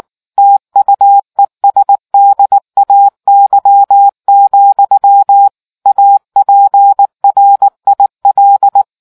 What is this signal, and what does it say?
Morse code, heard as a keyed tone, about 19 wpm, 800 Hz. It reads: TUESDAY, APRIL